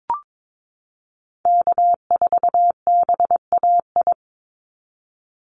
Morse code, heard as a keyed tone, about 22 wpm, 700 Hz. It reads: K4BAI
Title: Call Signs Repeated 3x - 22wpm